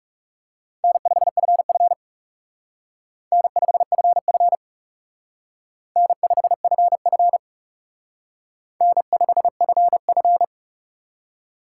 Morse code, heard as a keyed tone, about 38 wpm, 700 Hz. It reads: N5FF N5FF N5FF N5FF
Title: Speed Racing - USA 1x2 Call Signs 30wpm